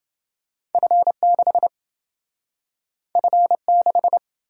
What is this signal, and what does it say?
Morse code, heard as a keyed tone, about 29 wpm, 700 Hz. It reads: F6 F6